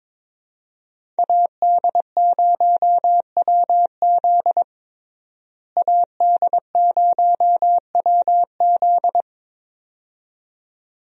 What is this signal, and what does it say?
Morse code, heard as a keyed tone, about 22 wpm, 700 Hz. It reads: AD0WZ AD0WZ
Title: USA Call Signs - 5 Characters - Repeated 2x - 22wpm